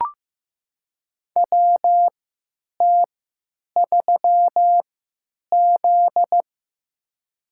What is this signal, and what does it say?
Morse code, heard as a keyed tone, about 15 wpm, spaced as for 8 wpm, 700 Hz. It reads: WT3Z
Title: USA Call Signs - 4 Characters - 3x Character Spacing - 15wpm